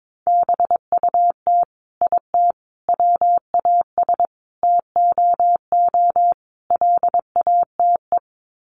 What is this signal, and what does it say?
Morse code, heard as a keyed tone, about 22 wpm, 700 Hz. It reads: BUT IT WAS TOO LATE
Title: Sets of 5 Words Encore2 22wpm